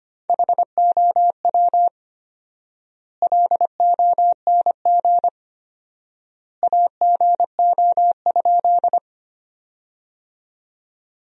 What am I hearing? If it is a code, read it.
Morse code, heard as a keyed tone, about 25 wpm, 700 Hz. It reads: HOW LONG AGO?
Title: 4x Word-Spacing - Sentences from Top 500 Words - 25wpm